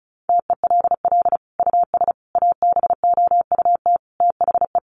Morse code, heard as a keyed tone, about 34 wpm, 700 Hz. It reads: TELL US ABOUT THE
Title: Sentences from Top 1200 Words 35wpm